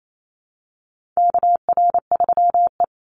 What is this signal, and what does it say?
Morse code, heard as a keyed tone, about 28 wpm, 700 Hz. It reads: KR3E
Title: Call Signs - 28wpm